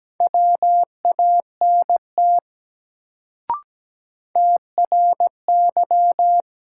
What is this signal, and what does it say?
Morse code, heard as a keyed tone, about 17 wpm, 700 Hz. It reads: WANT TRY
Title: Sets of 2 Words - New Word Drill - 17wpm